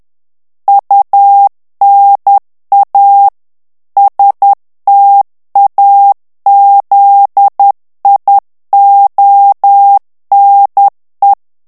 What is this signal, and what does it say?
Morse code, heard as a keyed tone, about 10 wpm, 800 Hz. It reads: UNA STAZIONE
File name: morse mp3 / ITALIEN texte clair 1 vitesse 10 mots minute